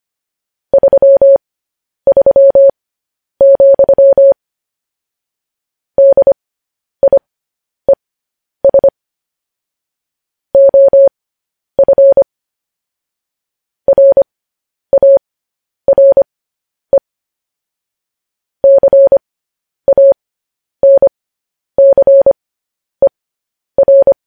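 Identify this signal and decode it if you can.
Morse code, heard as a keyed tone, about 25 wpm, spaced as for 10 wpm, 550 Hz. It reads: 33, DIES OF RARE CANCER
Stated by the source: Morse code news headlines at 10.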